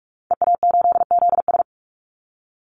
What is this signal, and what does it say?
Morse code, heard as a keyed tone, about 44 wpm, 700 Hz. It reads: EA8ZS
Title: Call Signs Repeated 2x - 45wpm